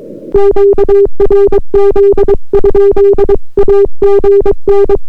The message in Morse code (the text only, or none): QRZ? AGN